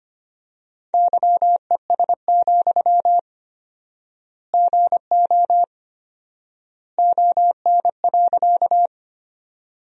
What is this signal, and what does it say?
Morse code, heard as a keyed tone, about 25 wpm, 700 Hz. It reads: YES, GO ON.
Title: 4x Word-Spacing - Sentences from Top 500 Words - 25wpm